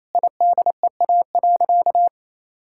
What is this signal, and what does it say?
Morse code, heard as a keyed tone, about 28 wpm, 700 Hz. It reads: IDEA.
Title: Sentences from Top 1000 Words - Review Part 6 - 28wpm